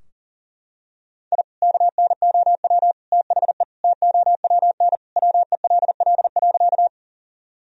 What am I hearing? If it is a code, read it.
Morse code, heard as a keyed tone, about 40 wpm, 700 Hz. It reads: I KNOW THE TOWN WELL.
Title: Sentences from Top 1000 Words - Review Part 2 - 40wpm